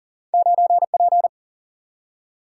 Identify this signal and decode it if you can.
Morse code, heard as a keyed tone, about 40 wpm, 700 Hz. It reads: 9 P
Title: Single-Letter Single-Number 40wpm